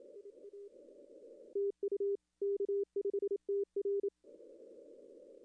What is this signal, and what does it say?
Morse code, heard as a keyed tone, about 27 wpm, 410 Hz. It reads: IA TU K5TR
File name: contests / arrl-160-k5tr-op-w5tn -2300